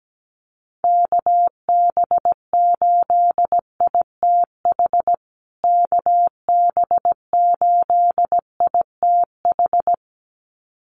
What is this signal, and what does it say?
Morse code, heard as a keyed tone, about 17 wpm, 700 Hz. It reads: KB8ITH KB8ITH
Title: USA Call Signs Repeated 2x 17wpm